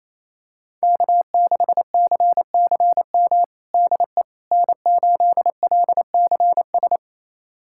Morse code, heard as a keyed tone, about 28 wpm, 700 Hz. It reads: K6CCM DE N8LCS
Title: USA Callsign DE Callsign - 5 Characters - 28wpm